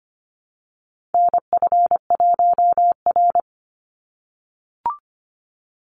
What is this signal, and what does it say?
Morse code, heard as a keyed tone, about 25 wpm, 700 Hz. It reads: NF1R
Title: NA SOTA Chaser Callsigns 25wpm